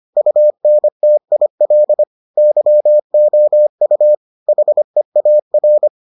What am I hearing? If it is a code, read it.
Morse code, heard as a keyed tone, about 25 wpm, 600 Hz. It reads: UNTIL YOU HEAR